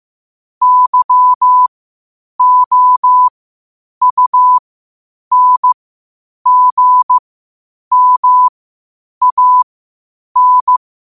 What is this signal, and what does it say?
Morse code, heard as a keyed tone, about 15 wpm, spaced as for 8 wpm, 1000 Hz. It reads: YOUNGMAN
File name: qotdfiles / 8wpm